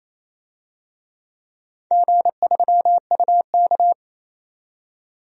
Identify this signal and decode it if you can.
Morse code, heard as a keyed tone, about 29 wpm, 700 Hz. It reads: G3UK
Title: DXCC-100 Call Signs - 4 Characters - 28wpm